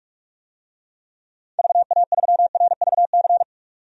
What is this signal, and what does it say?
Morse code, heard as a keyed tone, about 46 wpm, 700 Hz. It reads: VA3RVC
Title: DXCC-100 Call Signs - 6 Characters - 45wpm